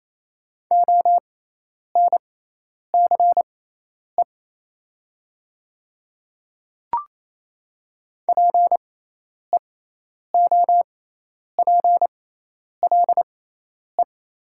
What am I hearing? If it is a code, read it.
Morse code, heard as a keyed tone, about 28 wpm, spaced as for 10 wpm, 690 Hz. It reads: ONCE PEOPLE
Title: 6x Character Spacing - Top 300 Words - 28wpm